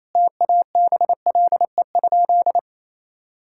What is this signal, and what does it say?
Morse code, heard as a keyed tone, about 28 wpm, 700 Hz. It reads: TABLE?
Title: Sentences from Top 1000 Words - Review Part 5 - 28wpm